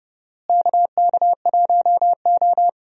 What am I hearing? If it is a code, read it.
Morse code, heard as a keyed tone, about 30 wpm, 700 Hz. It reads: KK1O
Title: USA Call Signs - 4 Characters - Repeated 3x - 30wpm